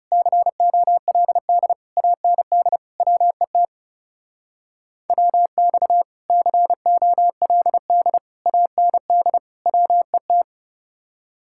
Morse code, heard as a keyed tone, about 35 wpm, 700 Hz. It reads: COLD AND WET WX COLD AND WET
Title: Speed-Racing Common QSO Phrases - 30wpm